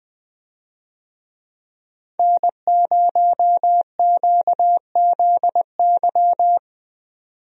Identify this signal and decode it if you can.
Morse code, heard as a keyed tone, about 20 wpm, 700 Hz. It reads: N0QZY
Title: USA Call Signs - 5 Characters - 20wpm